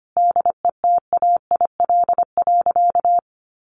Morse code, heard as a keyed tone, about 25 wpm, 700 Hz. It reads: DETAIL.
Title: Sentences from Top 1500 Words 25wpm